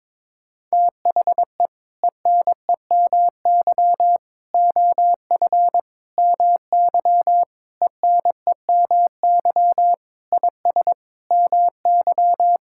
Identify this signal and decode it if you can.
Morse code, heard as a keyed tone, about 21 wpm, 700 Hz. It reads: THE ENEMY OF MY ENEMY IS MY
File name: practice / 22wpm / Sentences with 8-Word Phrases - 22wpm